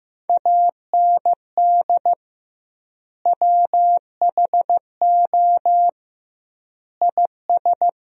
Morse code, heard as a keyed tone, about 15 wpm, 700 Hz. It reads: AND WHO IS
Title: 2x Word-Spacing - Sentences from Top 100 Words - 15wpm